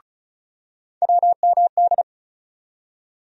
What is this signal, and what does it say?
Morse code, heard as a keyed tone, about 36 wpm, 700 Hz. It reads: WMD
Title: USA Call Sign Postfixes - 3 Letters 35wpm